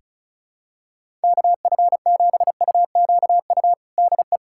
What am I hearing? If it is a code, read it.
Morse code, heard as a keyed tone, about 36 wpm, 700 Hz. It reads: KF7UQU DE
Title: USA Callsign DE Callsign - 6 Characters - 35wpm